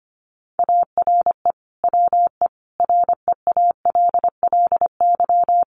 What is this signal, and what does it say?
Morse code, heard as a keyed tone, about 25 wpm, 700 Hz. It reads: ARE WE REALLY
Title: Sentences from Top 1400 Words 25wpm